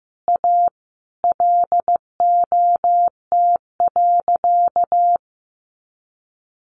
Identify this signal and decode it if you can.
Morse code, heard as a keyed tone, about 15 wpm, 700 Hz. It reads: A LOT.
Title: Easy Phrases 15wpm